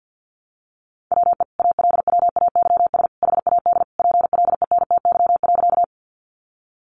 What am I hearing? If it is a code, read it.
Morse code, heard as a keyed tone, about 50 wpm, 700 Hz. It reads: WE ALWAYS HAD PLENTY.
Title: Sentences from Top 1300 Words 50wpm